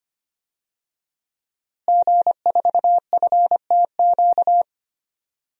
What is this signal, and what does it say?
Morse code, heard as a keyed tone, about 25 wpm, 700 Hz. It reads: G4FTQ
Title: DXCC-100 Call Signs - 5 Characters - Repeated 2x - 25wpm